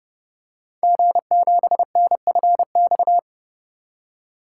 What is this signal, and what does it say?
Morse code, heard as a keyed tone, about 30 wpm, 700 Hz. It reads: G7NFX
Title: Code Groups - 5 Letter:Numbers - 30wpm